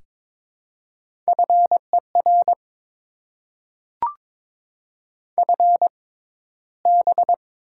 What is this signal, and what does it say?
Morse code, heard as a keyed tone, about 22 wpm, spaced as for 14 wpm, 700 Hz. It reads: FER FB
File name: Top 20 QSO Elements - 6x Character Spacing - 22wpm